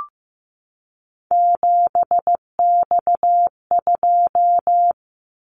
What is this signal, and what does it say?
Morse code, heard as a keyed tone, about 15 wpm, 700 Hz. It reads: 7X2
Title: Callsign Prefixes for Top 100 DXCC - 15wpm